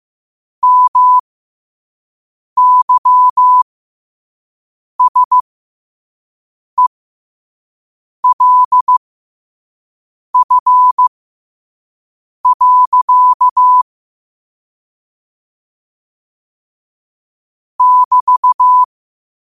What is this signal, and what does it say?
Morse code, heard as a keyed tone, about 15 wpm, spaced as for 5 wpm, 1000 Hz. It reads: MYSELF. <BT>
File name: qotdfiles / 5wpm